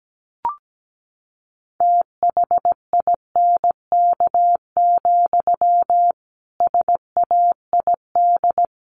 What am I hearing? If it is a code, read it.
Morse code, heard as a keyed tone, about 17 wpm, 700 Hz. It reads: THINK, SAID
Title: A Sound Of Thunder 17wpm